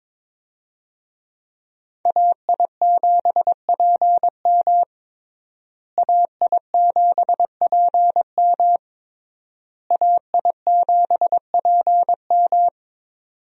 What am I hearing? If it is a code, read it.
Morse code, heard as a keyed tone, about 22 wpm, 700 Hz. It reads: AI7PM AI7PM AI7PM
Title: USA Call Signs - 5 Characters - Repeated 3x - 22wpm